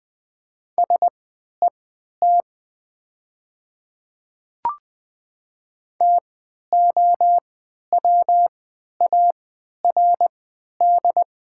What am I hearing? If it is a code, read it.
Morse code, heard as a keyed tone, about 20 wpm, spaced as for 11 wpm, 700 Hz. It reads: SET TOWARD
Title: 3x Character Spacing - Top 500 Words - 20wpm